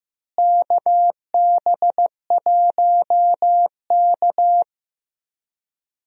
Morse code, heard as a keyed tone, about 15 wpm, 700 Hz. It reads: KB1K
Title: USA Call Signs - 4 Characters - Repeated 2x - 15wpm